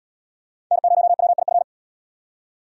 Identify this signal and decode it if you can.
Morse code, heard as a keyed tone, about 74 wpm, 700 Hz. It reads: N0WIF